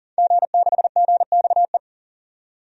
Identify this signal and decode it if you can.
Morse code, heard as a keyed tone, about 40 wpm, 700 Hz. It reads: G6GXE
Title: DXCC-100 Call Signs - 5 Characters - 40wpm